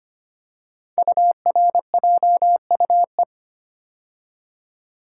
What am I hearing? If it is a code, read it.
Morse code, heard as a keyed tone, about 25 wpm, 690 Hz. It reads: URJUE
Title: Code Groups - 5 Letters - 25wpm